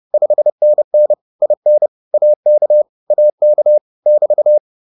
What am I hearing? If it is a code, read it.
Morse code, heard as a keyed tone, about 30 wpm, 600 Hz. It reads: 5NN IN AK AK <BT>